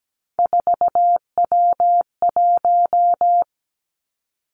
Morse code, heard as a keyed tone, about 17 wpm, 700 Hz. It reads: 4W1
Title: Three Challenging Letter-Number Combinations 17wpm